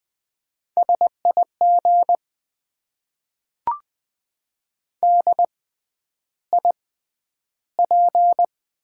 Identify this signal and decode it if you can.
Morse code, heard as a keyed tone, about 20 wpm, spaced as for 10 wpm, 700 Hz. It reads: SIG DIP
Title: Top 90 QSO Elements - 6x Character Spacing - 20wpm